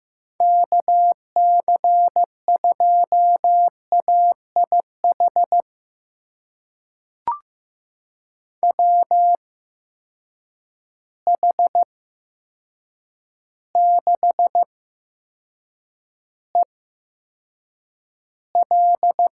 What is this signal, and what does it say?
Morse code, heard as a keyed tone, about 15 wpm, spaced as for 7 wpm, 700 Hz. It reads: KC2AIH WH6EL
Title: USA Call Signs - 6 Characters - 8x Character Spacing - 15wpm